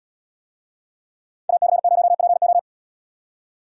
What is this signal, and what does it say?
Morse code, heard as a keyed tone, about 77 wpm, 700 Hz. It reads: AC0PO